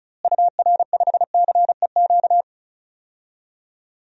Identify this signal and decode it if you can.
Morse code, heard as a keyed tone, about 36 wpm, 700 Hz. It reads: UR5CEQ
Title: DXCC-100 Call Signs - 6 Characters - Repeated 2x - 35wpm